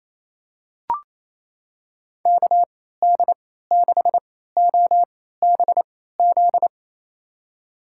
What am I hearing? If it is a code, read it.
Morse code, heard as a keyed tone, about 28 wpm, spaced as for 16 wpm, 700 Hz. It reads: KD6OBZ
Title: USA Call Signs - 3x Character Spacing - 28wpm